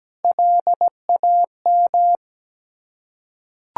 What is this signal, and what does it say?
Morse code, heard as a keyed tone, about 17 wpm, 690 Hz. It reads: LAM
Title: Ontario QSO Party Exchange 17wpm